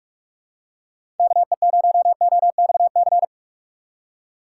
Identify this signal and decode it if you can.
Morse code, heard as a keyed tone, about 44 wpm, 700 Hz. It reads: KE0OXC
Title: USA Call Signs - 6 Characters - 6x Character Spacing - 45wpm